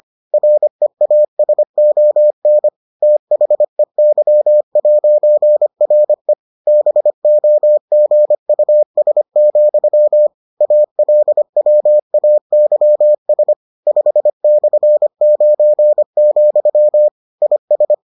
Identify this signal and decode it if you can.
Morse code, heard as a keyed tone, about 25 wpm, 600 Hz. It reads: REASON THEY'RE BOGUS, ALWAYS 5/9, IS